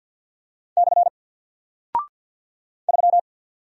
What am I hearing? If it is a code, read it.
Morse code, heard as a keyed tone, about 48 wpm, 700 Hz. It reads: / 3
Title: Single Character (Letter:Number:Punctuation:Prosign) Alt 50wpm